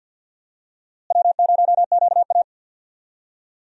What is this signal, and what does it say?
Morse code, heard as a keyed tone, about 51 wpm, 700 Hz. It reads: W0QA
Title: USA Call Signs - 4 Characters - Repeated 3x - 50wpm